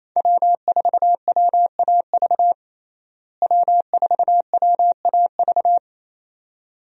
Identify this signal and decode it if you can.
Morse code, heard as a keyed tone, about 28 wpm, 700 Hz. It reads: W4WAV W4WAV
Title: USA Call Signs - 5 Characters - Repeated 2x - 28wpm